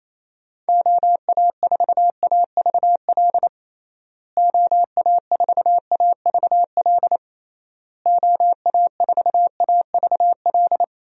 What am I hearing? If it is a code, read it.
Morse code, heard as a keyed tone, about 28 wpm, 700 Hz. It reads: OA4AVL OA4AVL OA4AVL
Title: USA Call Signs - 6 Characters - Repeated 3x - 28wpm